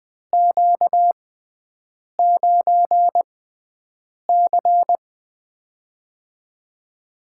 Morse code, heard as a keyed tone, about 20 wpm, spaced as for 7 wpm, 700 Hz. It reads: Q9C
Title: USA Call Signs - 6x Character Spacing - 20wpm